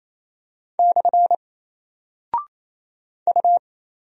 Morse code, heard as a keyed tone, about 27 wpm, 700 Hz. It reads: / U
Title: Single Character (Letter:Number:Punctuation:Prosign) 28wpm